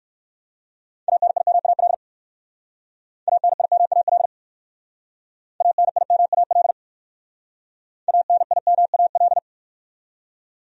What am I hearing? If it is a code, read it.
Morse code, heard as a keyed tone, about 56 wpm, 700 Hz. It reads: ANIMAL ANIMAL ANIMAL ANIMAL
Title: Speed-Racing Top 200 Words 45wpm